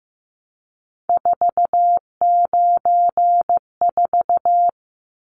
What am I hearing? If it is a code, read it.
Morse code, heard as a keyed tone, about 15 wpm, 700 Hz. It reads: 494
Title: Numbers - 3-Digit 15wpm